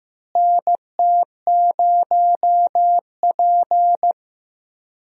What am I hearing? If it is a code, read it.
Morse code, heard as a keyed tone, about 15 wpm, 700 Hz. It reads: NT0P
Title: USA Call Signs - 4 Characters - Repeated 3x - 15wpm